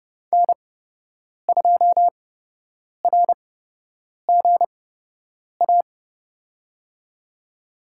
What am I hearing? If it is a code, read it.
Morse code, heard as a keyed tone, about 32 wpm, spaced as for 8 wpm, 700 Hz. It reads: N2RGA
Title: USA Call Signs - 5 Characters - 8x Character Spacing - 30wpm